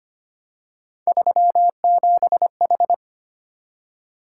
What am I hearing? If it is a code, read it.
Morse code, heard as a keyed tone, about 25 wpm, 700 Hz. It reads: 37H
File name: Three Letter-Number Combinations - 25wpm